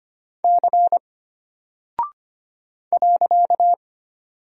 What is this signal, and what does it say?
Morse code, heard as a keyed tone, about 25 wpm, 700 Hz. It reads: C .
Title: Single Character (Letter:Number:Punctuation:Prosign) Alt 25wpm